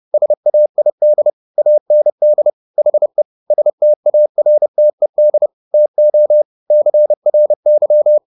Morse code, heard as a keyed tone, about 30 wpm, 600 Hz. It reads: SAID AND HE STARTED TO CRY